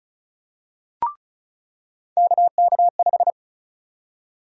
Morse code, heard as a keyed tone, about 34 wpm, 700 Hz. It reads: KK5
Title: USA Call Sign Prefixes - 35wpm